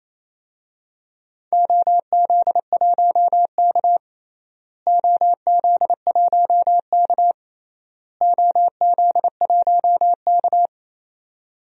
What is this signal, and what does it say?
Morse code, heard as a keyed tone, about 28 wpm, 700 Hz. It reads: OZ1K OZ1K OZ1K
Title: DXCC-100 Call Signs - 4 Characters - Repeated 3x - 28wpm